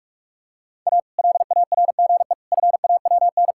Morse code, heard as a keyed tone, about 41 wpm, 700 Hz. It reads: A LARGE FAWN
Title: Pangram Challenge v2 - 45wpm